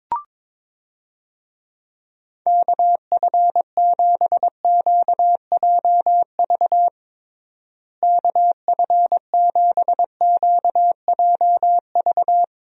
Morse code, heard as a keyed tone, about 22 wpm, 700 Hz. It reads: KF7QJV KF7QJV
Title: USA Call Signs - 6 Characters - Repeated 3x - 22wpm